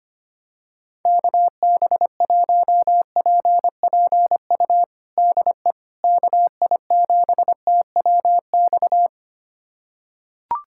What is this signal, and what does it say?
Morse code, heard as a keyed tone, about 25 wpm, 700 Hz. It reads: KB1PPU DE KI7TWX
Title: USA Callsign DE Callsign - 6 Characters - 25wpm